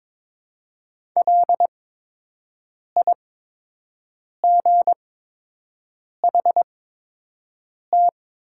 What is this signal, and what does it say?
Morse code, heard as a keyed tone, about 22 wpm, spaced as for 6 wpm, 700 Hz. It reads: LIGHT
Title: 8x Character Spacing - Top 300 Words - 22wpm